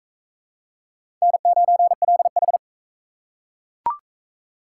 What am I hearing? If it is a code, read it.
Morse code, heard as a keyed tone, about 41 wpm, 700 Hz. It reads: N9LH